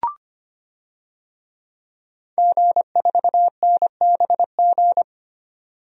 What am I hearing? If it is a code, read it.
Morse code, heard as a keyed tone, about 25 wpm, 700 Hz. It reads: G4NBG
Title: DXCC-100 Call Signs - 5 Characters - 25wpm